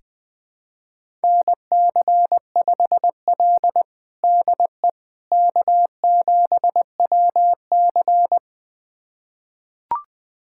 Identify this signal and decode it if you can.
Morse code, heard as a keyed tone, about 20 wpm, 700 Hz. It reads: NC5L DE K7WC
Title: USA Callsign DE Callsign - 4 Characters - 20wpm